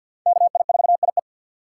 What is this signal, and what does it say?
Morse code, heard as a keyed tone, about 47 wpm, 700 Hz. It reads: KI4IE
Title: USA Call Signs - 5 Characters - 6x Character Spacing - 50wpm